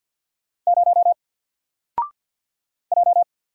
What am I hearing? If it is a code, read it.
Morse code, heard as a keyed tone, about 48 wpm, 700 Hz. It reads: 0 J
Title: Single Letter-Number 50wpm